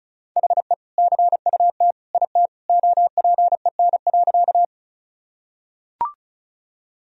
Morse code, heard as a keyed tone, about 36 wpm, 700 Hz. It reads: HE CUT IT OPEN.
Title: Sentences from Top 1000 Words - Review Part 1 - 35wpm